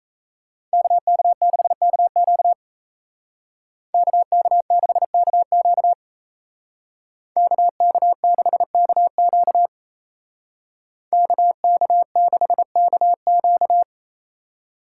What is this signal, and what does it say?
Morse code, heard as a keyed tone, about 33 wpm, 700 Hz. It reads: KK6KQ KK6KQ KK6KQ KK6KQ
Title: Speed Racing - USA 2x2 Call Signs 28wpm